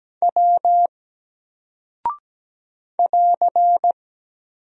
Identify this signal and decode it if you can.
Morse code, heard as a keyed tone, about 17 wpm, 700 Hz. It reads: W <AR>
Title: Single Character (Letter:Number:Punctuation:Prosign) Alt 17wpm